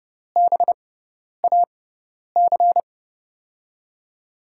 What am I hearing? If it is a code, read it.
Morse code, heard as a keyed tone, about 30 wpm, spaced as for 10 wpm, 700 Hz. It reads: BAC
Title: USA Call Signs - 6 Characters - 6x Character Spacing - 30wpm